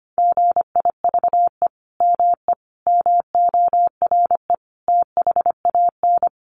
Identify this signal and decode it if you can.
Morse code, heard as a keyed tone, about 25 wpm, 700 Hz. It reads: GIVE ME MORE THAN
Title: Sentences from Top 200 Words 25wpm